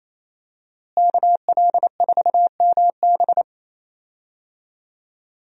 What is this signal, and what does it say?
Morse code, heard as a keyed tone, about 28 wpm, 700 Hz. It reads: KL4MB
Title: USA Call Signs - 5 Characters - Repeated 2x - 28wpm